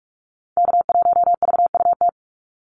Morse code, heard as a keyed tone, about 44 wpm, 700 Hz. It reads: K1VUT
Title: Call Signs - 45wpm